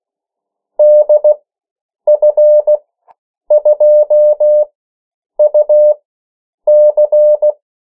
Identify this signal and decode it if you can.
Morse code, heard as a keyed tone, about 16 wpm, spaced as for 8 wpm, 600 Hz. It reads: DF2UC
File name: morse mp3 / 14 urok 13